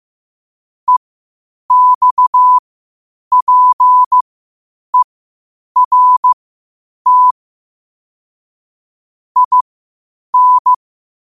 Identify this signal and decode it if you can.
Morse code, heard as a keyed tone, about 15 wpm, spaced as for 8 wpm, 1000 Hz. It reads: EXPERT IN
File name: qotdfiles / 8wpm